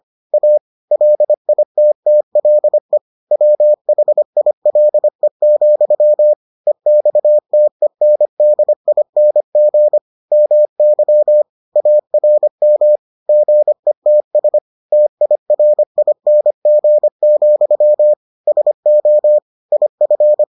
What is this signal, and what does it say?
Morse code, heard as a keyed tone, about 25 wpm, 600 Hz. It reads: A LITTLE WHILE, EXTENDING MY ARM GETS TIRING, SO IF